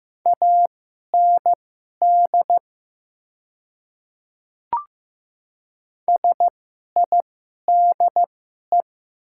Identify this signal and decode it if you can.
Morse code, heard as a keyed tone, about 15 wpm, spaced as for 11 wpm, 700 Hz. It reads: AND SIDE
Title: 2x Character Spacing - Top 100 Words - 15wpm